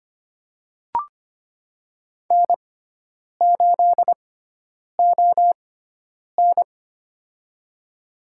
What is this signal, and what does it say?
Morse code, heard as a keyed tone, about 25 wpm, spaced as for 9 wpm, 700 Hz. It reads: N8ON
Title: USA Call Signs - 4 Characters - 6x Character Spacing - 25wpm